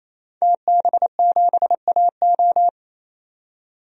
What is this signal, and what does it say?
Morse code, heard as a keyed tone, about 28 wpm, 700 Hz. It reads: TB7AO
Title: DXCC-100 Call Signs - 5 Characters - 28wpm